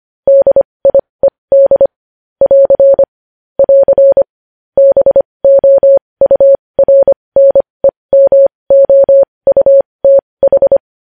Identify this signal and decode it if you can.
Morse code, heard as a keyed tone, about 25 wpm, spaced as for 20 wpm, 550 Hz. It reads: DIED <AR> <AR> BOURNEMOUTH
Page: news-headlines-in-morse-code-at-20-wpm